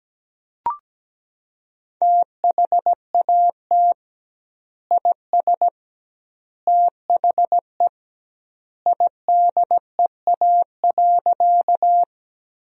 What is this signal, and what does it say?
Morse code, heard as a keyed tone, about 17 wpm, 700 Hz. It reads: THAT IS THE IDEA.
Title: 2x Word-Spacing - Sentences from Top 300 Words - 17wpm